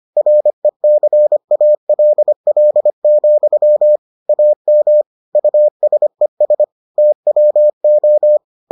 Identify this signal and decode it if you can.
Morse code, heard as a keyed tone, about 25 wpm, 600 Hz. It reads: RECALL, AM USES TWO